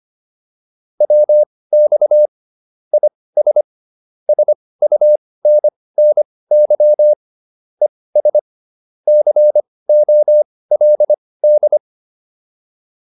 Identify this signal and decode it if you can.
Morse code, heard as a keyed tone, about 25 wpm, spaced as for 18 wpm, 600 Hz. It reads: WX IS SUNNY ES COLD